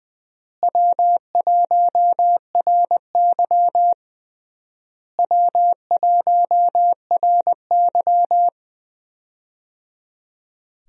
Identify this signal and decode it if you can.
Morse code, heard as a keyed tone, about 21 wpm, 700 Hz. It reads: W1RY W1RY
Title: USA Call Signs - 4 Characters - Repeated 3x - 20wpm